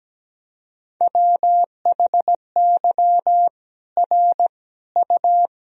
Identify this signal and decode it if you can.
Morse code, heard as a keyed tone, about 17 wpm, 700 Hz. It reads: WHY R U
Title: Sentences from Top 200 Words and Abbreviations 17wpm